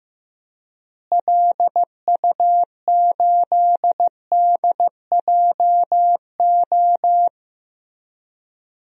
Morse code, heard as a keyed tone, about 15 wpm, 700 Hz. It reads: LU8DJO
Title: DXCC-100 Call Signs - 6 Characters - Repeated 3x - 15wpm